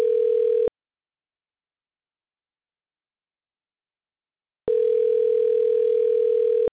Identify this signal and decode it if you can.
Ringback tone (North America), 440 + 480 Hz.